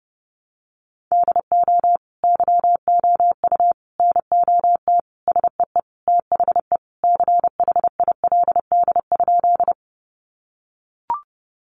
Morse code, heard as a keyed tone, about 30 wpm, 700 Hz. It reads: DO YOU NOT SEE THE CHILD?
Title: Sentences from Top 800 Words 30wpm